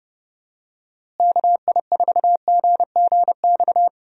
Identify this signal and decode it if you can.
Morse code, heard as a keyed tone, about 32 wpm, 700 Hz. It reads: KI4GGX
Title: USA Call Signs - 6 Characters - Repeated 3x - 30wpm